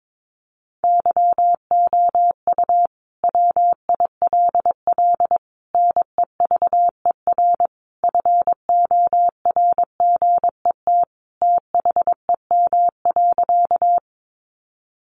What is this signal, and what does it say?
Morse code, heard as a keyed tone, about 22 wpm, 700 Hz. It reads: YOU WILL NEVER FORGET THEM.
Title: Sentences from Top 1200 Words 22wpm